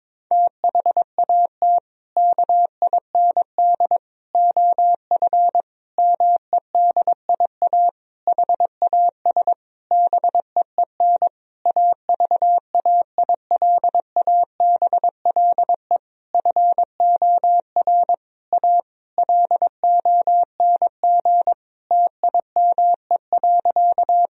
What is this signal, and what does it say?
Morse code, heard as a keyed tone, about 22 wpm, 700 Hz. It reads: THAT KIND OF MEDIA HAS BEEN AVAILABLE FOR A LONG TIME.
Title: Sentences from Top 1000 Words - Review Part 4 - 22wpm